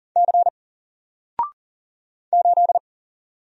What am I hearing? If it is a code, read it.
Morse code, heard as a keyed tone, about 40 wpm, 700 Hz. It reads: C 8
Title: Single Letter-Number 40wpm